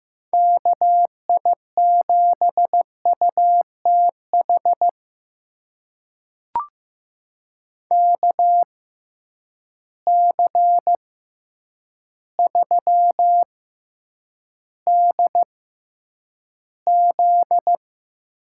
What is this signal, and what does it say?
Morse code, heard as a keyed tone, about 15 wpm, spaced as for 8 wpm, 700 Hz. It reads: KI7UTH KC3DZ
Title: USA Call Signs - 6 Characters - 6x Character Spacing - 15wpm